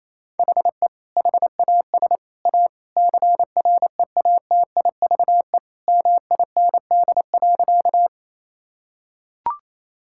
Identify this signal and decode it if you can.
Morse code, heard as a keyed tone, about 28 wpm, 700 Hz. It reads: HE HAS A CREATIVE MIND.
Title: Sentences from Top 1000 Words - Review Part 5 - 28wpm